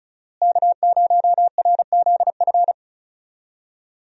Morse code, heard as a keyed tone, about 36 wpm, 700 Hz. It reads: K0RZF
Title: USA Call Signs - 5 Characters - Repeated 2x - 35wpm